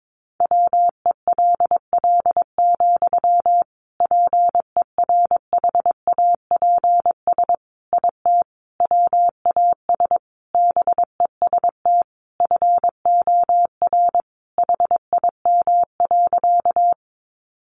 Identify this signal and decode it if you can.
Morse code, heard as a keyed tone, about 22 wpm, 700 Hz. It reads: WELL, PERHAPS IT WAS BEST FOR HIM.
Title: Sentences from Top 500 Words 22wpm